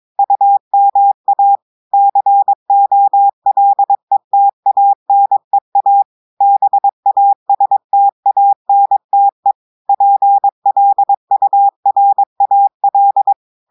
Morse code, heard as a keyed tone, about 22 wpm, 800 Hz. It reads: UMA COLETANEA BASTANTE PLURAL
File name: audios / portugais / portugais texte en clair 1 vitesse 22 mots minutes